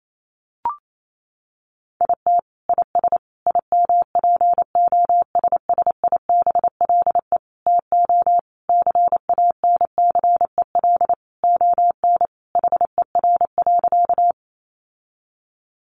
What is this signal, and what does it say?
Morse code, heard as a keyed tone, about 28 wpm, 700 Hz. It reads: IT IS IMPOSSIBLE TO CANCEL ON HER.
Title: Sentences from Top 1200 Words 28wpm